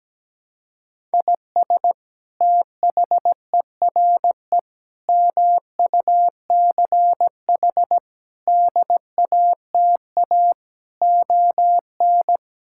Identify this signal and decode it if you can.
Morse code, heard as a keyed tone, about 17 wpm, 700 Hz. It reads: IS THERE MUCH DATA ON
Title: Sentences from Top 1000 Words - Review Part 6 - 17wpm